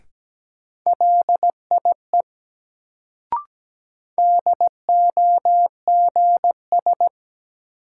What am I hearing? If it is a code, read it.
Morse code, heard as a keyed tone, about 17 wpm, 700 Hz. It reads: LIE DOGS
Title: Sentences from Top 1600 Words - New Word Drill - 17wpm